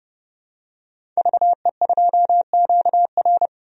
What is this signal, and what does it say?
Morse code, heard as a keyed tone, about 30 wpm, 700 Hz. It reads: VE2QR
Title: DXCC-100 Call Signs - 5 Characters - 30wpm